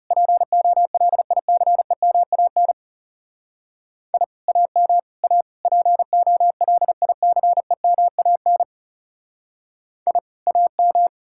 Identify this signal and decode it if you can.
Morse code, heard as a keyed tone, about 36 wpm, 700 Hz. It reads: POLICEMAN I AM A POLICEMAN I AM
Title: Speed-Racing Common QSO Phrases - 30wpm